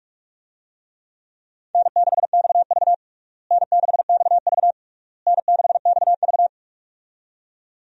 Morse code, heard as a keyed tone, about 46 wpm, 700 Hz. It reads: N6XV N6XV N6XV
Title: USA Call Signs - 4 Characters - Repeated 3x - 45wpm